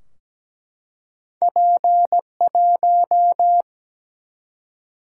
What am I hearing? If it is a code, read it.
Morse code, heard as a keyed tone, about 17 wpm, 700 Hz. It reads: P1
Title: Single-Letter Single-Number 17wpm